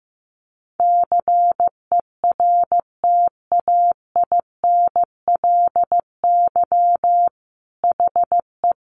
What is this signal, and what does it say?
Morse code, heard as a keyed tone, about 15 wpm, 700 Hz. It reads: CERTAINLY HE
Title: Sentences from Top 900 Words 15wpm